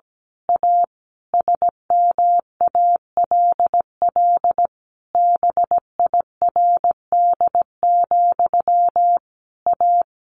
Morse code, heard as a keyed tone, about 17 wpm, 700 Hz. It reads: A SMALL BIRD, A
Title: A Sound Of Thunder 17wpm